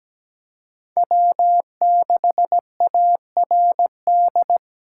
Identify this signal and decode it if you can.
Morse code, heard as a keyed tone, about 20 wpm, 700 Hz. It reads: W6ARD
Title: Speed Racing - USA 1x3 Call Signs 17wpm